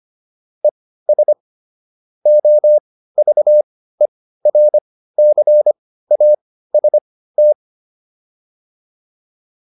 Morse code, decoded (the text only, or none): ES OVERCAST